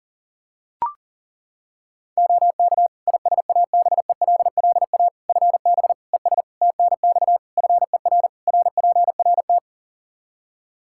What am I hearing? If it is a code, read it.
Morse code, heard as a keyed tone, about 40 wpm, 700 Hz. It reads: OK ISABELLA FB ES TNX FER RPRT
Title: Practice QSO - Exchange 2 - Info - 40wpm